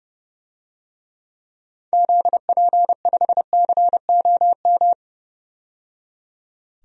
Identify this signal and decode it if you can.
Morse code, heard as a keyed tone, about 30 wpm, 700 Hz. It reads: ZP5COM
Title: DXCC-100 Call Signs - 6 Characters - 30wpm